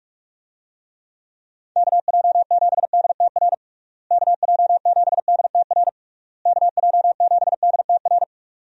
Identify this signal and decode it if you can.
Morse code, heard as a keyed tone, about 41 wpm, 700 Hz. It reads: KJ7DTR KJ7DTR KJ7DTR
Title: USA Call Signs - 6 Characters - Repeated 3x - 45wpm